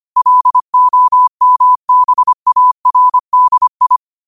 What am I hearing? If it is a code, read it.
Morse code, heard as a keyed tone, about 25 wpm, 1000 Hz. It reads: LOMBARDI
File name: qotdfiles / 25wpm